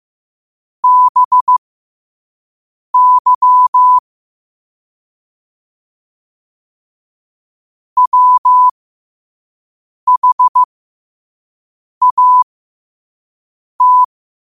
Morse code, heard as a keyed tone, about 15 wpm, spaced as for 5 wpm, 1000 Hz. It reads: BY WHAT
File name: qotdfiles / 5wpm